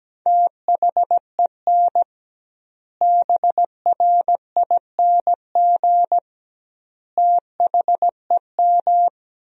Morse code, heard as a keyed tone, about 17 wpm, 700 Hz. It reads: THEN BRING THEM
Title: 2x Word-Spacing - Sentences from Top 500 Words - 17wpm